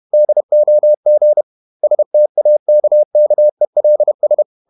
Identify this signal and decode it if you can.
Morse code, heard as a keyed tone, about 31 wpm, 600 Hz. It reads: DOG STAKKELS
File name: Ferie QRQ 31wpm